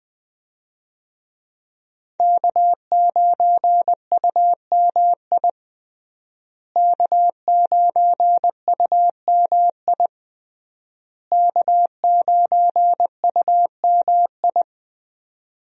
Morse code, heard as a keyed tone, about 20 wpm, 700 Hz. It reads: K9UMI K9UMI K9UMI
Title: USA Call Signs - 5 Characters - Repeated 3x - 20wpm